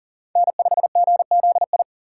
Morse code, heard as a keyed tone, about 40 wpm, 700 Hz. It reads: N5GZI
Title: USA Call Signs - 4x Character Spacing - 40wpm